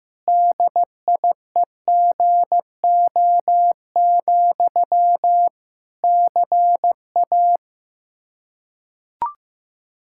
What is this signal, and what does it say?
Morse code, heard as a keyed tone, about 15 wpm, 700 Hz. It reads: DIEGO, CA
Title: Practice QSO - Exchange 1 - Beginning - 15wpm